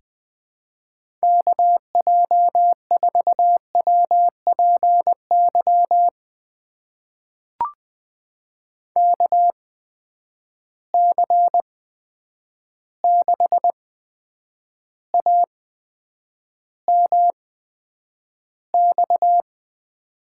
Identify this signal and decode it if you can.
Morse code, heard as a keyed tone, about 20 wpm, spaced as for 9 wpm, 700 Hz. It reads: KJ4WPY KC6AMX
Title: USA Call Signs - 8x Character Spacing - 20wpm